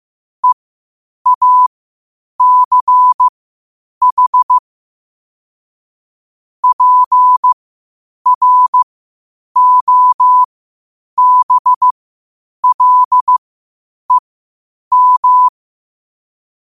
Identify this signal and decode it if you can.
Morse code, heard as a keyed tone, about 15 wpm, spaced as for 8 wpm, 1000 Hz. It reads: EACH PROBLEM
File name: qotdfiles / 8wpm